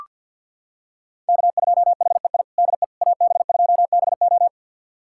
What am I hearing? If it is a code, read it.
Morse code, heard as a keyed tone, about 47 wpm, 700 Hz. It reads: K2HEI DE AB2BO
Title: USA Callsign DE Callsign - 5 Characters - 50wpm